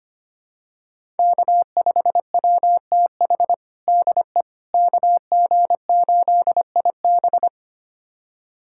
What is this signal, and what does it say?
Morse code, heard as a keyed tone, about 25 wpm, 700 Hz. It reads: K5WTH DE KG8IB
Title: USA Callsign DE Callsign - 5 Characters - 25wpm